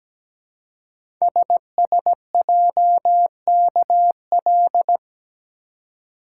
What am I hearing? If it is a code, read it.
Morse code, heard as a keyed tone, about 17 wpm, 700 Hz. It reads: SSJKL
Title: Code Groups - 5 Letters - 17wpm